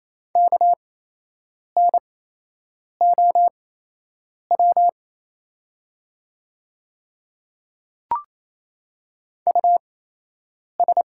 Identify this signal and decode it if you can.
Morse code, heard as a keyed tone, about 28 wpm, spaced as for 8 wpm, 700 Hz. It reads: KNOW US
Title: 8x Character Spacing - Top 200 Words - 28wpm